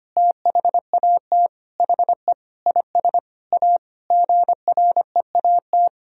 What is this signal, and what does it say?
Morse code, heard as a keyed tone, about 25 wpm, 700 Hz. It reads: THAT HE IS A GREAT
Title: Sentences from Top 1000 Words - Review Part 3 - 25wpm